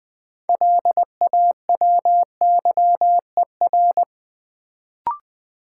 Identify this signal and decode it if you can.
Morse code, heard as a keyed tone, about 20 wpm, 700 Hz. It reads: LAWYER
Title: Build A Word - 2 Letters At A Time - 6-Letter Words - 20wpm